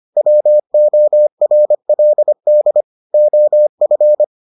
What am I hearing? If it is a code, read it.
Morse code, heard as a keyed tone, about 25 wpm, 600 Hz. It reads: WORLD OF